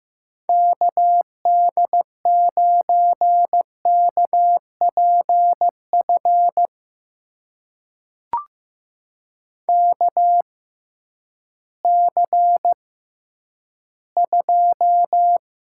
Morse code, heard as a keyed tone, about 15 wpm, spaced as for 10 wpm, 700 Hz. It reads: KD9KPF KC2
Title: USA Call Signs - 6 Characters - 6x Character Spacing - 15wpm